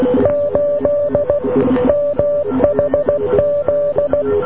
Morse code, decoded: 8 MHZ